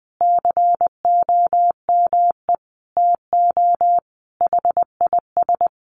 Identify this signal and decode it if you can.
Morse code, heard as a keyed tone, about 20 wpm, 700 Hz. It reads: COME TO HIS
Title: Sentences from Top 1000 Words 20wpm